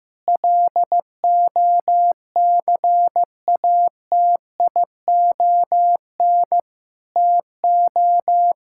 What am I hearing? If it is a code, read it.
Morse code, heard as a keyed tone, about 15 wpm, 700 Hz. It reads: LOCATION TO
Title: Sentences from Top 1000 Words - Review Part 6 - 15wpm